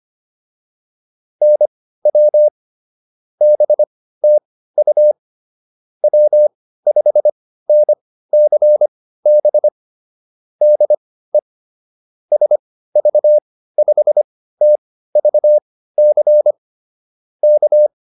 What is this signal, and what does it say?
Morse code, heard as a keyed tone, about 25 wpm, spaced as for 15 wpm, 600 Hz. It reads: NW BTU W5NCB DE SV5TVC K